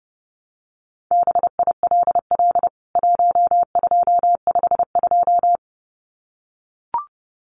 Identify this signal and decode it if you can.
Morse code, heard as a keyed tone, about 30 wpm, 700 Hz. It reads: BILL 1252
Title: CWT Exchange 30wpm